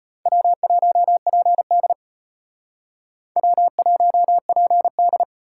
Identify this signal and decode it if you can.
Morse code, heard as a keyed tone, about 32 wpm, 700 Hz. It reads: W1PD W1PD